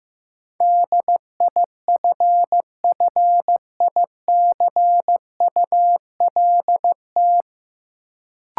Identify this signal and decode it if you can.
Morse code, heard as a keyed tone, about 15 wpm, 700 Hz. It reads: DIFFICULT
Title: Sets of 4 Words - New Word Drill - 15wpm